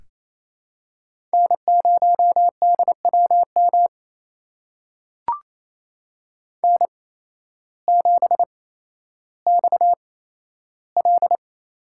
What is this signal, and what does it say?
Morse code, heard as a keyed tone, about 28 wpm, spaced as for 13 wpm, 700 Hz. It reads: N0DWM N7XL
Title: USA Call Signs - 5 Characters - 8x Character Spacing - 28wpm